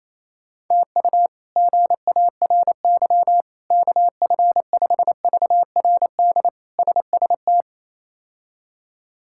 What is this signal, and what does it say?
Morse code, heard as a keyed tone, about 28 wpm, 700 Hz. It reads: TU GARY KF5VRD SST
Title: K1USN SST QSO Practice - 28wpm